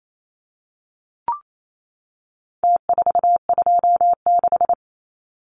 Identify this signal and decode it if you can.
Morse code, heard as a keyed tone, about 28 wpm, 700 Hz. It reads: T426
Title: Numbers - Contest Style - 4-Digit 28wpm